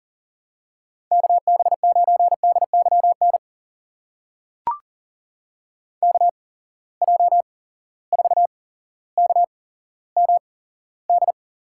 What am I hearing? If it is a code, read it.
Morse code, heard as a keyed tone, about 40 wpm, spaced as for 17 wpm, 700 Hz. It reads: KB9DYN KJ4KMD
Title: USA Call Signs - 6 Characters - 8x Character Spacing - 40wpm